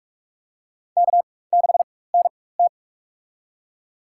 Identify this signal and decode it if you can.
Morse code, heard as a keyed tone, about 46 wpm, spaced as for 21 wpm, 700 Hz. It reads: K6NT
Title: USA Call Signs - 4 Characters - 4x Character Spacing - 45wpm